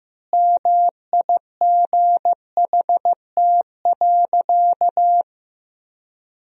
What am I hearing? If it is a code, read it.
Morse code, heard as a keyed tone, about 15 wpm, 700 Hz. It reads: MIGHT.
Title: 4x Word-Spacing - Sentences from Top 500 Words - 15wpm